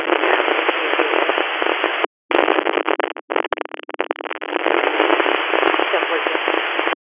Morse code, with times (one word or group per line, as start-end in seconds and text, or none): none